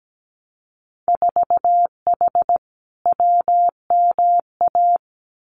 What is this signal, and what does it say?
Morse code, heard as a keyed tone, about 17 wpm, 700 Hz. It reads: 4H WMA
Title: Winter Field Day Exchanges Encore 17wpm